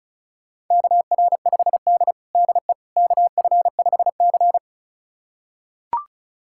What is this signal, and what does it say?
Morse code, heard as a keyed tone, about 36 wpm, 700 Hz. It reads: KR5D DE KF5C
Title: USA Callsign DE Callsign - 4 Characters - 35wpm